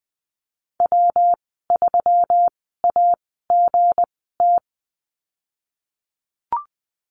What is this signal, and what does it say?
Morse code, heard as a keyed tone, about 20 wpm, spaced as for 15 wpm, 700 Hz. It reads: W3AGT
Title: 213 Lesson 34g - Number 8 - Call Signs 20wpm